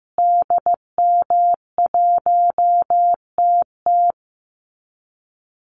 Morse code, heard as a keyed tone, about 15 wpm, 700 Hz. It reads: DM1TT
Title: Call Signs Repeated 3x - 15wpm